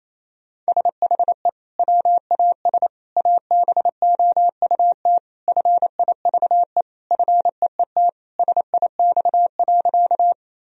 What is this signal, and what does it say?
Morse code, heard as a keyed tone, about 28 wpm, 700 Hz. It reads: SHE WAS ABOUT FIVE FEET SIX.
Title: Sentences from Top 1000 Words - Review Part 2 - 28wpm